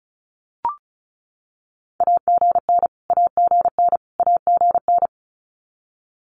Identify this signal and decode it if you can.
Morse code, heard as a keyed tone, about 34 wpm, 700 Hz. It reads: AGN AGN AGN
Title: Abbreviations Repeated 3x - 35wpm